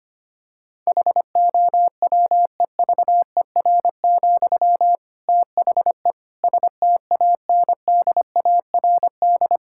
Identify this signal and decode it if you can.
Morse code, heard as a keyed tone, about 25 wpm, 700 Hz. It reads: HOWEVER, THE STANDARD
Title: Sentences from Top 1000 Words - Review Part 6 - 25wpm